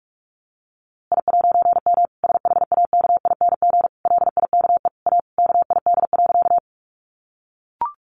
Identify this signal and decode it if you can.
Morse code, heard as a keyed tone, about 44 wpm, 700 Hz. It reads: I'M SHAKING LIKE A KID.
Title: A Sound Of Thunder 45wpm